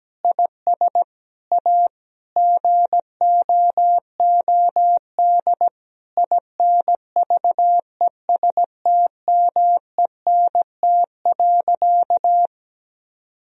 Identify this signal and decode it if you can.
Morse code, heard as a keyed tone, about 17 wpm, 700 Hz. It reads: IS A GOOD INVESTMENT.
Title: Sentences from Top 1000 Words - Review Part 4 - 17wpm